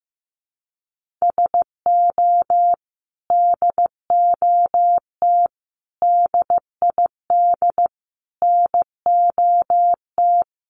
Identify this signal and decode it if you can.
Morse code, heard as a keyed tone, about 15 wpm, 700 Hz. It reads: SO DOT DID NOT
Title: Sentences from Top 1500 Words 15wpm